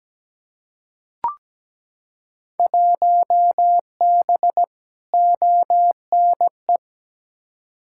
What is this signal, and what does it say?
Morse code, heard as a keyed tone, about 17 wpm, 700 Hz. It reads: 1B ONE
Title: Field Day Exchanges - 17wpm